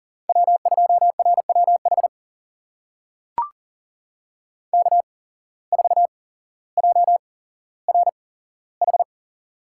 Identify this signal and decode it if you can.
Morse code, heard as a keyed tone, about 40 wpm, spaced as for 17 wpm, 700 Hz. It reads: W2RWH K4JRH
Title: USA Call Signs - 5 Characters - 8x Character Spacing - 40wpm